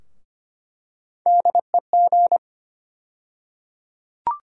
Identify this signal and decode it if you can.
Morse code, heard as a keyed tone, about 25 wpm, 700 Hz. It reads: DEG
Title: USA Call Sign Postfixes - 3 Letters 25wpm